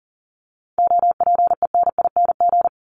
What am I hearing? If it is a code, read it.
Morse code, heard as a keyed tone, about 40 wpm, 700 Hz. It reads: OPENING
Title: ING Suffix - Top 100 Words with Base Words 40wpm